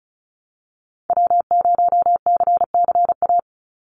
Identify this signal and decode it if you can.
Morse code, heard as a keyed tone, about 36 wpm, 700 Hz. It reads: W0CCA
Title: NA SOTA Activator Callsigns 35wpm